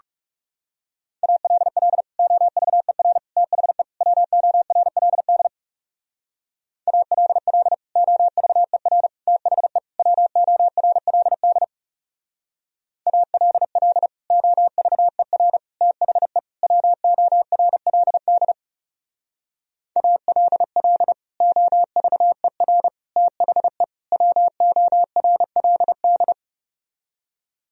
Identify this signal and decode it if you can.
Morse code, heard as a keyed tone, about 36 wpm, 700 Hz. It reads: ALL OVER THE WORLD ALL OVER THE WORLD ALL OVER THE WORLD ALL OVER THE WORLD
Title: Speed-Racing - Sets of 4 Words 30wpm